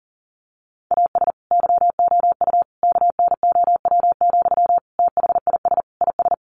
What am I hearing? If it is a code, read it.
Morse code, heard as a keyed tone, about 40 wpm, 700 Hz. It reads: AS YOU KNOW, THIS IS
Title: Sentences from Top 1100 Words 40wpm